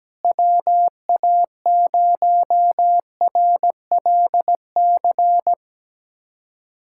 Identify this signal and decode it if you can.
Morse code, heard as a keyed tone, about 17 wpm, 700 Hz. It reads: WA0RLC
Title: USA Callsign DE Callsign 17wpm